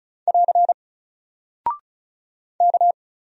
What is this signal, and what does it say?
Morse code, heard as a keyed tone, about 36 wpm, 700 Hz. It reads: <AR> K
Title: Single Character (Letter:Number:Punctuation:Prosign) Alt 35wpm